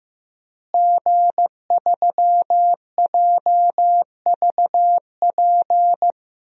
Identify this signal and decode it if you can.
Morse code, heard as a keyed tone, about 15 wpm, 700 Hz. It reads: G3JVP
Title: DXCC-100 Call Signs - 5 Characters - Repeated 2x - 15wpm